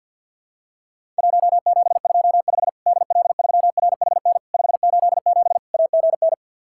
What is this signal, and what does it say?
Morse code, heard as a keyed tone, about 50 wpm, 700 Hz. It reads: 1725 DL3RHN 587 AGN
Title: Worked All Europe DX Contest - QTC Exchange with Repeats - 50wpm